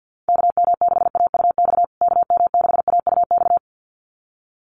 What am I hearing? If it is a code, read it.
Morse code, heard as a keyed tone, about 50 wpm, 700 Hz. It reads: KM6AUX KM6AUX
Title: USA Call Signs Repeated 2x 50wpm